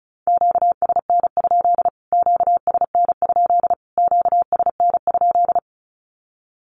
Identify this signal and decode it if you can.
Morse code, heard as a keyed tone, about 34 wpm, 700 Hz. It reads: QSN? QSN? QSN?
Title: Q-Codes Repeated 3x 35wpm